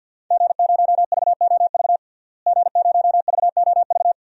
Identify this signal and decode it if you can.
Morse code, heard as a keyed tone, about 48 wpm, 700 Hz. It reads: G0VOV G0VOV
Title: DXCC-100 Call Signs - 5 Characters - Repeated 2x - 50wpm